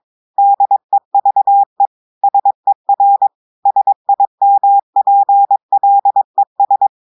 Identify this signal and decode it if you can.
Morse code, heard as a keyed tone, about 22 wpm, 800 Hz. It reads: DEVE SER SIMPLES
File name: audios / portugais / portugais texte en clair 2 vitesse 22 mots minutes